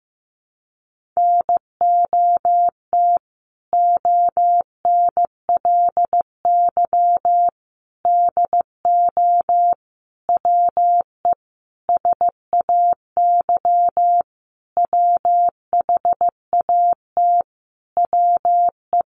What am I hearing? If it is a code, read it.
Morse code, heard as a keyed tone, about 15 wpm, 700 Hz. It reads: NOT ONLY DO WE SAY WHAT WE
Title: Sentences from Top 1200 Words 15wpm